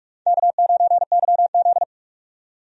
Morse code, heard as a keyed tone, about 46 wpm, 700 Hz. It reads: K9YZ
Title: USA Call Signs - 4 Characters - 45wpm